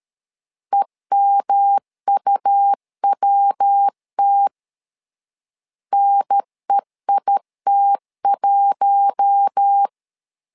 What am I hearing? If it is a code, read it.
Morse code, heard as a keyed tone, about 13 wpm, 800 Hz. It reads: EMUWT NEIT1
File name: LEZIONI / 4 Q1 12wpm 2028